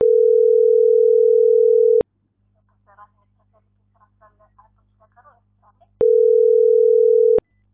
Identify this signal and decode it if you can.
Ringback tone (North America), 440 + 480 Hz.